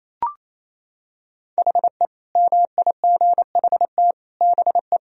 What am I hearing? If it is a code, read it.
Morse code, heard as a keyed tone, about 28 wpm, 700 Hz. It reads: HE MIGHT BE
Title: Sentences from Top 1000 Words - Review Part 2 - 28wpm